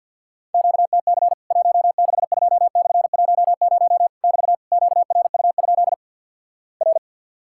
Morse code, heard as a keyed tone, about 50 wpm, 700 Hz. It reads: QTC 162/10 <BT> QRU? R
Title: Worked All Europe DX Contest - QTC Exchange with Repeats - 50wpm